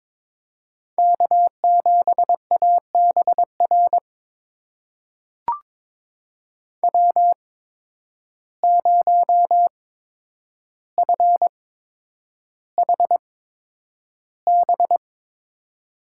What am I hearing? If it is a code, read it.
Morse code, heard as a keyed tone, about 22 wpm, spaced as for 9 wpm, 700 Hz. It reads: K7ABR W0FHB
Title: USA Call Signs - 5 Characters - 8x Character Spacing - 22wpm